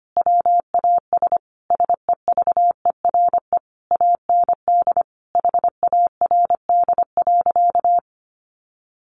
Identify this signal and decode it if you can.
Morse code, heard as a keyed tone, about 25 wpm, 700 Hz. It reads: WAS SEVERE AND HARD.
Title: Sentences from Top 1400 Words 25wpm